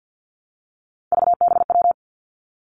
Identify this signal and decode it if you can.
Morse code, heard as a keyed tone, about 50 wpm, 700 Hz. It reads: VBW
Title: Three Challenging Letter-Number Combinations 50wpm